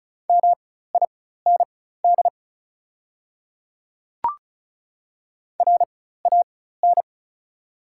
Morse code, heard as a keyed tone, about 34 wpm, spaced as for 16 wpm, 700 Hz. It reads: MIND RAN
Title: 4x Character Spacing - Top 500 Words - 35wpm